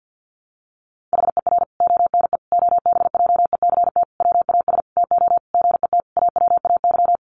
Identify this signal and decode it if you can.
Morse code, heard as a keyed tone, about 50 wpm, 700 Hz. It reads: HER ONE OBJECT WAS TO GET AWAY
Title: Sentences from Top 1500 Words 50wpm